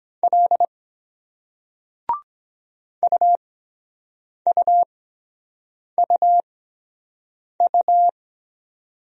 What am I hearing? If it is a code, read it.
Morse code, heard as a keyed tone, about 23 wpm, spaced as for 7 wpm, 700 Hz. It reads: L UUUU